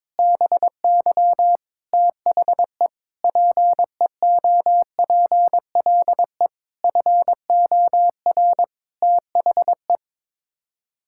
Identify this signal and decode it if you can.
Morse code, heard as a keyed tone, about 22 wpm, 700 Hz. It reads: BY THE PEOPLE FOR THE
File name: Sets of 9 Words - 22wpm